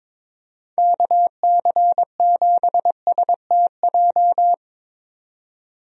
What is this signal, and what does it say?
Morse code, heard as a keyed tone, about 22 wpm, 700 Hz. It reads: KC7STJ
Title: USA Call Signs - 6 Characters - 22wpm